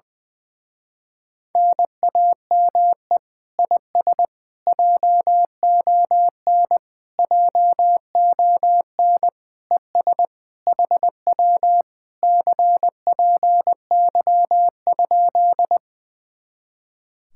Morse code, heard as a keyed tone, about 20 wpm, 700 Hz. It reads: NAME IS JON JON ES HW CPY?
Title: Speed-Racing Common QSO Phrases Encore - 20wpm